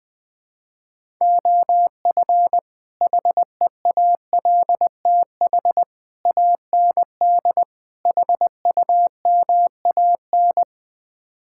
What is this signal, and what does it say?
Morse code, heard as a keyed tone, about 20 wpm, 700 Hz. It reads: OF HEALTH AND HUMAN
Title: Build A Phrase - 5-Word Phrases - 20wpm